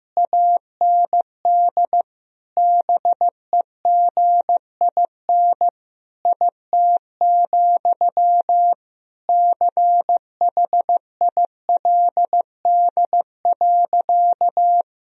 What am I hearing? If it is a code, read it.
Morse code, heard as a keyed tone, about 15 wpm, 700 Hz. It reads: AND BEGIN IT, CHILD.
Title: Sentences from Top 1000 Words - Review Part 1 - 15wpm